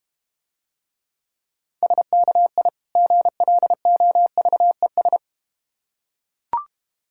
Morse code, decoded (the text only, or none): SKI GLOVES